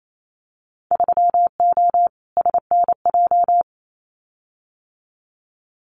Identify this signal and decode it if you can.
Morse code, heard as a keyed tone, about 28 wpm, 700 Hz. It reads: 3O SNJ
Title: Winter Field Day Exchanges Encore 28wpm